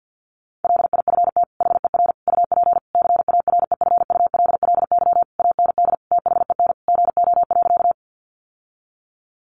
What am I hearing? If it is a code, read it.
Morse code, heard as a keyed tone, about 50 wpm, 700 Hz. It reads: LIFT HER UP CAREFULLY AND THEN GO.
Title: Sentences from Top 1300 Words 50wpm